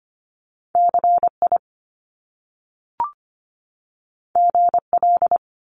Morse code, heard as a keyed tone, about 25 wpm, 700 Hz. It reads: CI GL
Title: Top Two Letter Combinations in Words 25wpm